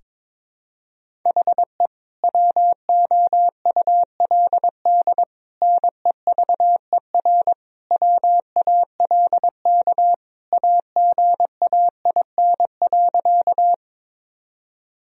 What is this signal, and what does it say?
Morse code, heard as a keyed tone, about 22 wpm, 700 Hz. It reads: HE WOULD NEVER WALK AGAIN.
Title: Sentences from Top 1000 Words - Review Part 1 - 22wpm